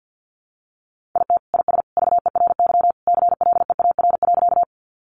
Morse code, heard as a keyed tone, about 50 wpm, 700 Hz. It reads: IT IS VERY CLEAR.
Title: Sentences from Top 500 Words 50wpm